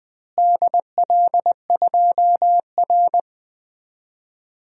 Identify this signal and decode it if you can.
Morse code, heard as a keyed tone, about 20 wpm, 700 Hz. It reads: DL2R
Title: DXCC-100 Call Signs - 4 Characters - Repeated 3x - 20wpm